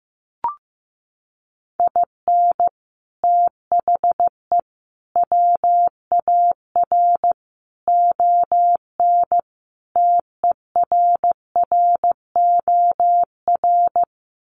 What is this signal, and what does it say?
Morse code, heard as a keyed tone, about 15 wpm, 700 Hz. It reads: IN THE WAR ON TERROR
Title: Sets of 5 Words Encore2 15wpm